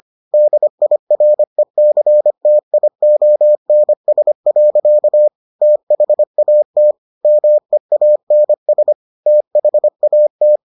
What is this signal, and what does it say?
Morse code, heard as a keyed tone, about 25 wpm, 600 Hz. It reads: DIRECTIONS. THAT MEANS THAT